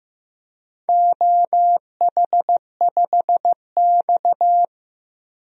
Morse code, heard as a keyed tone, about 15 wpm, 700 Hz. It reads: OH5X
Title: DXCC-100 Call Signs - 4 Characters - 15wpm